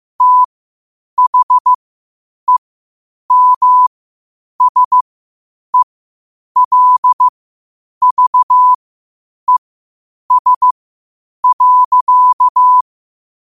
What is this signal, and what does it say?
Morse code, heard as a keyed tone, about 15 wpm, spaced as for 8 wpm, 1000 Hz. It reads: THEMSELVES.
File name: qotdfiles / 8wpm